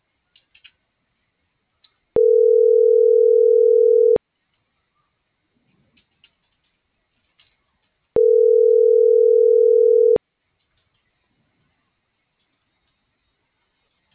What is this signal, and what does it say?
Ringback tone (North America), 440 + 480 Hz.